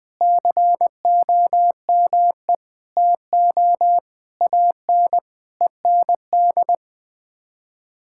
Morse code, heard as a keyed tone, about 20 wpm, 700 Hz. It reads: COME TO AN END
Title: Sets of 8 Words - 20wpm